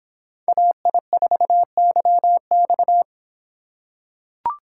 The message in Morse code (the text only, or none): AI4YX